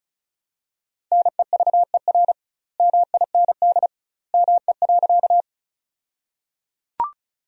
Morse code, heard as a keyed tone, about 34 wpm, 700 Hz. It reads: NEVER MIND ME.
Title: 2x Word-Spacing - Sentences from Top 500 Words - 35wpm